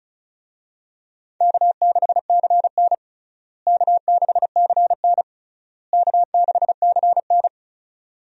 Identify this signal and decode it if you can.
Morse code, heard as a keyed tone, about 36 wpm, 700 Hz. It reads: K6CN K6CN K6CN
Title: USA Call Signs - 4 Characters - Repeated 3x - 35wpm